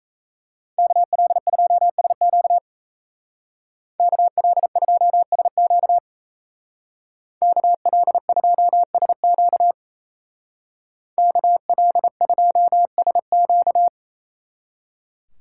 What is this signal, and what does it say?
Morse code, heard as a keyed tone, about 34 wpm, 700 Hz. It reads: KL2SQ KL2SQ KL2SQ KL2SQ
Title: Speed Racing - USA 2x2 Call Signs 28wpm